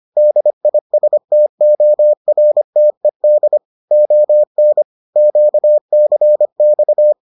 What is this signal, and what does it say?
Morse code, heard as a keyed tone, about 25 wpm, 600 Hz. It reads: DISTORTED ON QCX